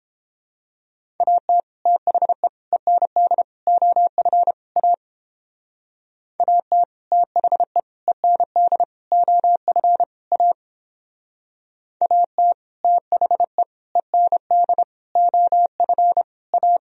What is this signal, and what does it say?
Morse code, heard as a keyed tone, about 31 wpm, 700 Hz. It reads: AT THE END OF A AT THE END OF A AT THE END OF A